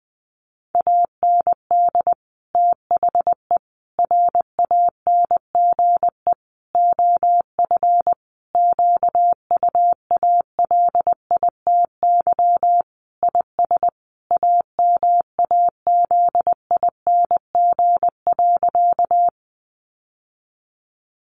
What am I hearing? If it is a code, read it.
Morse code, heard as a keyed tone, about 20 wpm, 700 Hz. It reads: AND THE RANGE OF QUALITY IS AMAZING.
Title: Sentences from Top 1500 Words 20wpm